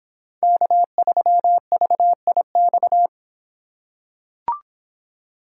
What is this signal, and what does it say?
Morse code, heard as a keyed tone, about 25 wpm, 700 Hz. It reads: K3VIX